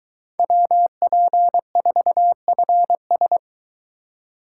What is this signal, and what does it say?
Morse code, heard as a keyed tone, about 24 wpm, 700 Hz. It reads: WP4FS